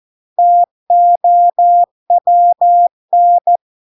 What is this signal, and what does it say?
Morse code, heard as a keyed tone, about 14 wpm, 700 Hz. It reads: TOWN